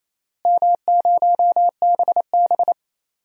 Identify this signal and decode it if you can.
Morse code, heard as a keyed tone, about 28 wpm, 700 Hz. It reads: M0BB
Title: DXCC-100 Call Signs - 4 Characters - Repeated 3x - 28wpm